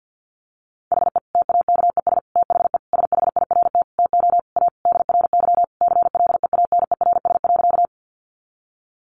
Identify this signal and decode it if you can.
Morse code, heard as a keyed tone, about 50 wpm, 700 Hz. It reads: HE TAKES THE SHIRT TO A DRY CLEANERS.
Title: Additional Easy Phrases 50wpm